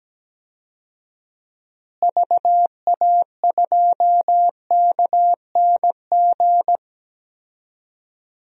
Morse code, heard as a keyed tone, about 17 wpm, 700 Hz. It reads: VA2KNG
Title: DXCC-100 Call Signs - 6 Characters - Repeated 2x - 17wpm